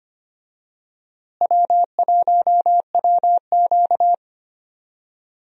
Morse code, heard as a keyed tone, about 25 wpm, 700 Hz. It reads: W1WQ
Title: USA Call Signs - 4 Characters - Repeated 2x - 25wpm